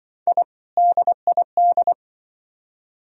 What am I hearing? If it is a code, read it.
Morse code, heard as a keyed tone, about 25 wpm, 700 Hz. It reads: I DID
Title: Speed-Racing - Sets of 2 Words 20wpm